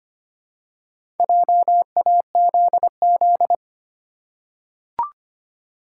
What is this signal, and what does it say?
Morse code, heard as a keyed tone, about 25 wpm, 700 Hz. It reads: JAZZ
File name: practice / 25wpm / Easy Words with Hints - 25wpm